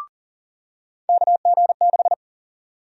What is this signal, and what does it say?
Morse code, heard as a keyed tone, about 40 wpm, 700 Hz. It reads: KG6
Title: USA Call Sign Prefixes - 40wpm